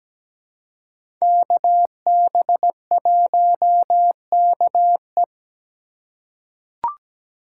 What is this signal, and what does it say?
Morse code, heard as a keyed tone, about 17 wpm, 700 Hz. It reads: KB1KE
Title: USA Call Signs - 5 Characters - 2x Character Spacing - 17wpm